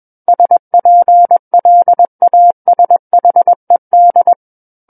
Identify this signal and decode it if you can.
Morse code, heard as a keyed tone, about 21 wpm, 700 Hz. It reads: SPLASHED
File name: Book One / 12-Chapter 12 - 21 WPM